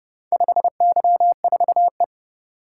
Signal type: Morse code

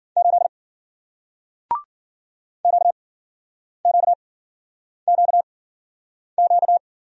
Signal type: Morse code

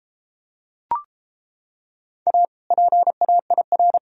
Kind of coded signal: Morse code

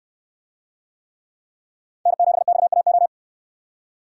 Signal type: Morse code

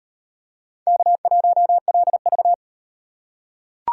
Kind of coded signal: Morse code

